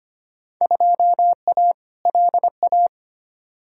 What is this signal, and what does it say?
Morse code, heard as a keyed tone, about 25 wpm, 700 Hz. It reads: 2A LA
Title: Field Day Exchanges - 25wpm